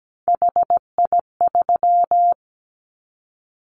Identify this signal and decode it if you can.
Morse code, heard as a keyed tone, about 17 wpm, 700 Hz. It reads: HI3
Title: Callsign Prefixes for Top 100 DXCC - 17wpm